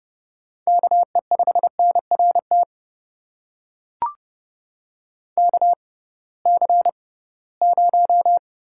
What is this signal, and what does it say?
Morse code, heard as a keyed tone, about 30 wpm, spaced as for 19 wpm, 700 Hz. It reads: KE5NRT KC0
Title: USA Call Signs - 6 Characters - 6x Character Spacing - 30wpm